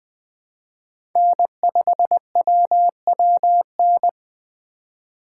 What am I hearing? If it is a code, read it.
Morse code, heard as a keyed tone, about 20 wpm, 700 Hz. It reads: N5WWN
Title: Speed Racing - USA Call Signs 17wpm